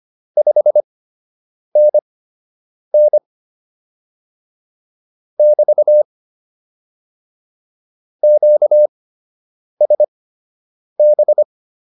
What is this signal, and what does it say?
Morse code, heard as a keyed tone, about 25 wpm, spaced as for 8 wpm, 600 Hz. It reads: 5NN <BT> QSB